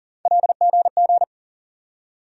Morse code, heard as a keyed tone, about 40 wpm, 700 Hz. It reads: LGG
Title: USA Call Sign Postfixes - 3 Letters 40wpm